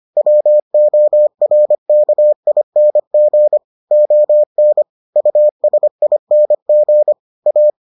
Morse code, heard as a keyed tone, about 25 wpm, 600 Hz. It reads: WORKING ON USING A